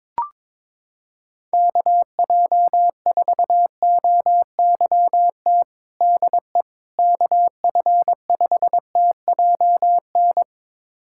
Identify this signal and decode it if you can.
Morse code, heard as a keyed tone, about 22 wpm, 700 Hz. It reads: KJ4OYT DE KF5TJN
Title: USA Callsign DE Callsign - 6 Characters - 22wpm